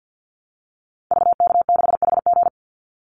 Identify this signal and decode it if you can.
Morse code, heard as a keyed tone, about 50 wpm, 700 Hz. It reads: VK6HG
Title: Call Signs Repeated 2x - 50wpm